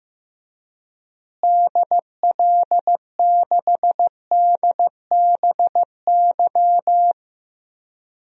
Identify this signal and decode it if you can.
Morse code, heard as a keyed tone, about 15 wpm, 700 Hz. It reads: DL6DBY
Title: DXCC-100 Call Signs - 6 Characters - Repeated 3x - 15wpm